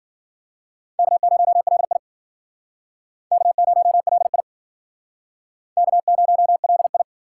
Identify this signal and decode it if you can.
Morse code, heard as a keyed tone, about 51 wpm, 700 Hz. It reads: K0LI K0LI K0LI